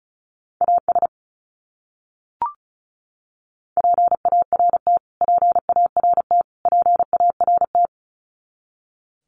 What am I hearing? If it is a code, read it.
Morse code, heard as a keyed tone, about 34 wpm, 700 Hz. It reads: AS PART PART PART
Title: Top 200 Words Repeated 35wpm